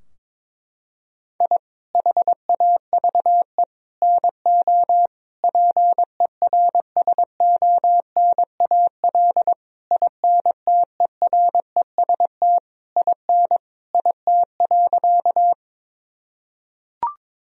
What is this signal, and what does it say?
Morse code, heard as a keyed tone, about 22 wpm, 700 Hz. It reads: I HAVE NO PERSONAL INTEREST IN IT.
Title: Sentences from Top 1000 Words - Review Part 4 - 22wpm